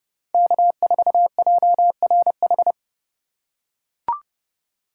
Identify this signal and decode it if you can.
Morse code, heard as a keyed tone, about 30 wpm, 700 Hz. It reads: K4JRH
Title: USA Call Signs - 5 Characters - 2x Character Spacing - 30wpm